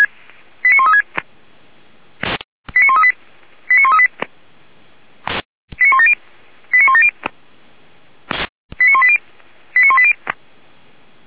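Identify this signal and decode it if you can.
Five-tone selective call (ZVEI), heard as quick runs of five tones, tones of about 70 ms. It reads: 89136 89138 89138 89170 89170 89180 89180
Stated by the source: ZVEI five-tone selective call received on an SDR by the Feuerwehr Orlamünde fire service.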